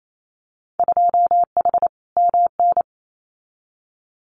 Morse code, heard as a keyed tone, about 28 wpm, 700 Hz. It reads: 2H MN
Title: Winter Field Day Exchanges 28wpm